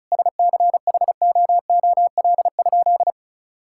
Morse code, heard as a keyed tone, about 36 wpm, 700 Hz. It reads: SCHOOL?
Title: Sentences from Top 1000 Words - Review Part 5 - 35wpm